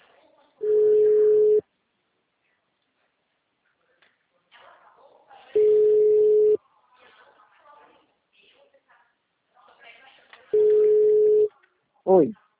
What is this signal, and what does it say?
Ringback tone (Europe, 425 Hz), with speech.
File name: out- -2004- - - .45222